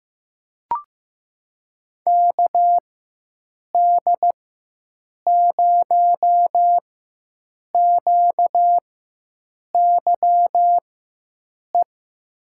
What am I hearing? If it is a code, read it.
Morse code, heard as a keyed tone, about 15 wpm, spaced as for 7 wpm, 700 Hz. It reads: KD0QYE
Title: USA Call Signs - 6 Characters - 4x Character Spacing - 15wpm